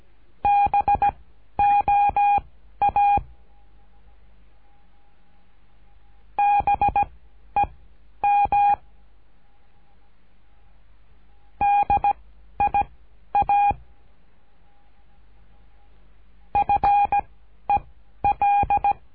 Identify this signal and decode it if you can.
Morse code, heard as a keyed tone, about 17 wpm, spaced as for 11 wpm, 810 Hz. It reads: BOA BEM DIA FEL
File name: cursos / CURSO DE TELEGRAFIA / Telegrafia - Trilha 13